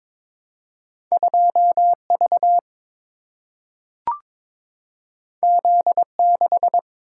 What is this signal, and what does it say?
Morse code, heard as a keyed tone, about 22 wpm, 700 Hz. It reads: 2 V Z 6
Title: Single-Letter Single-Number 22wpm